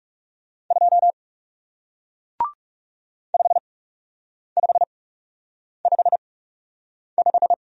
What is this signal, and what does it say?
Morse code, heard as a keyed tone, about 37 wpm, spaced as for 8 wpm, 700 Hz. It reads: 2 5555
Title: Speed-Racing Single Letter-Number 30wpm